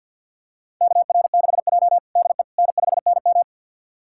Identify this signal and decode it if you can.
Morse code, heard as a keyed tone, about 48 wpm, 700 Hz. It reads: KR6J DE N5NM
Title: USA Callsign DE Callsign - 4 Characters - 50wpm